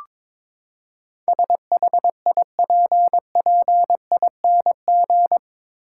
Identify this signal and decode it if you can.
Morse code, heard as a keyed tone, about 22 wpm, 700 Hz. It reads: SHIPPING
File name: practice / 22wpm / Eight Letter Words - 22wpm